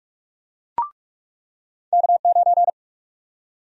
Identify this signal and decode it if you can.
Morse code, heard as a keyed tone, about 45 wpm, 700 Hz. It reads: K9
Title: USA Call Sign Prefixes - 45wpm